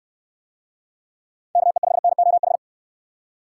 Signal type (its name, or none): Morse code